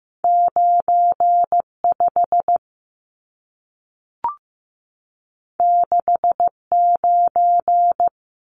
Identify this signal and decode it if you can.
Morse code, heard as a keyed tone, about 15 wpm, 700 Hz. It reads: 95 69
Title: Numbers - 2-Digit 15wpm